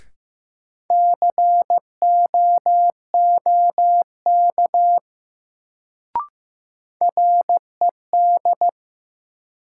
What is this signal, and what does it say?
Morse code, heard as a keyed tone, about 15 wpm, 700 Hz. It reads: COOK RED
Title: Sentences from Top 700 Words - New Word Drill - 15wpm